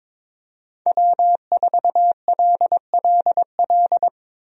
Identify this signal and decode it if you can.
Morse code, heard as a keyed tone, about 22 wpm, 700 Hz. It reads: W4LLL
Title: USA Call Signs - 5 Characters - 22wpm